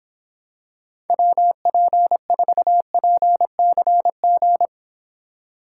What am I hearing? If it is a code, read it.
Morse code, heard as a keyed tone, about 26 wpm, 700 Hz. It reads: WP4PCG